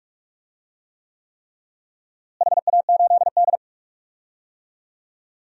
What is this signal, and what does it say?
Morse code, heard as a keyed tone, about 42 wpm, 700 Hz. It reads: HA8D
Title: DXCC-100 Call Signs - 4 Characters - Repeated 3x - 45wpm